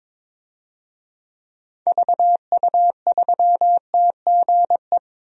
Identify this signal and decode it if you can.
Morse code, heard as a keyed tone, about 22 wpm, 700 Hz. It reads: VU3TGE
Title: DXCC-100 Call Signs - 6 Characters - Repeated 2x - 22wpm